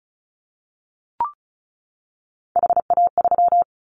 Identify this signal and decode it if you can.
Morse code, heard as a keyed tone, about 34 wpm, 700 Hz. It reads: HA3
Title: Callsign Prefixes for Top 25 DXCC - 35wpm